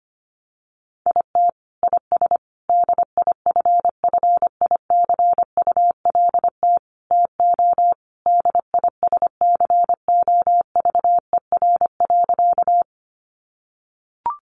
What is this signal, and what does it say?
Morse code, heard as a keyed tone, about 25 wpm, 700 Hz. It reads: IT IS DIFFICULT TO DISCOVER.
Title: Sentences from Top 1100 Words 25wpm